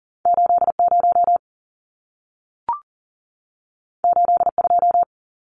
Morse code, heard as a keyed tone, about 40 wpm, 700 Hz. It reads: 80 82
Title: Numbers - 2-Digit 40wpm